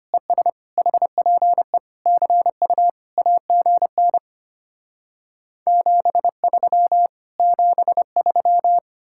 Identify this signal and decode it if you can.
Morse code, heard as a keyed tone, about 33 wpm, 700 Hz. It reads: ES HPE CU AGN 73 73